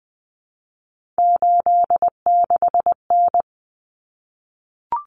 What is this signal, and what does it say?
Morse code, heard as a keyed tone, about 20 wpm, 700 Hz. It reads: 86N
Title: Common Cut Numbers - 3-Digit 20wpm